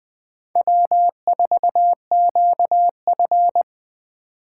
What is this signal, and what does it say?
Morse code, heard as a keyed tone, about 23 wpm, 700 Hz. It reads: W4QF
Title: Speed Racing - USA 1x2 Call Signs 20wpm